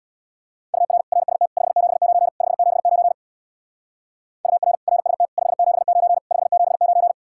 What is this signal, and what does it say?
Morse code, heard as a keyed tone, about 60 wpm, 700 Hz. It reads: UR RST 579 579 UR RST 579 579
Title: Speed-Racing Common QSO Phrases Encore - 50wpm